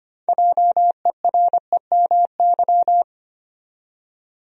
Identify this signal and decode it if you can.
Morse code, heard as a keyed tone, about 25 wpm, 700 Hz. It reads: JEREMY
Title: USA Common Names - 25wpm